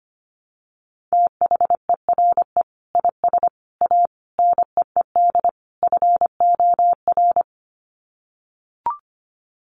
Morse code, heard as a keyed tone, about 25 wpm, 700 Hz. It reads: THERE IS A NEED FOR
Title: Sets of 5 Words Encore2 25wpm